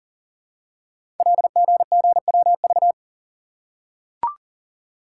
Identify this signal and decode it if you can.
Morse code, heard as a keyed tone, about 40 wpm, 700 Hz. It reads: LGGWV
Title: Code Groups - 5 Letters - 40wpm